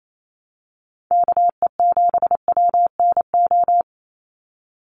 Morse code, heard as a keyed tone, about 28 wpm, 700 Hz. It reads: KE7WNO
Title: USA Call Signs 28wpm